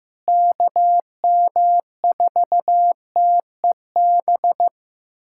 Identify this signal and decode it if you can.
Morse code, heard as a keyed tone, about 15 wpm, 700 Hz. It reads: KM4TEB
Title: USA Callsign DE Callsign 15wpm